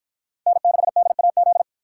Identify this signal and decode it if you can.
Morse code, heard as a keyed tone, about 50 wpm, 700 Hz. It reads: N6DAZ